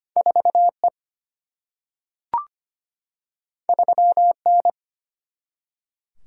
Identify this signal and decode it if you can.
Morse code, heard as a keyed tone, about 25 wpm, 700 Hz. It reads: 4E 3N
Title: Single-Letter Single-Number 25wpm